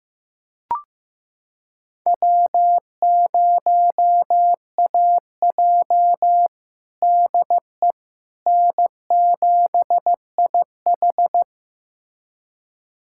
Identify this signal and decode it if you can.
Morse code, heard as a keyed tone, about 15 wpm, 700 Hz. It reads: W0AJ DE N7IH
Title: USA Callsign DE Callsign - 4 Characters - 15wpm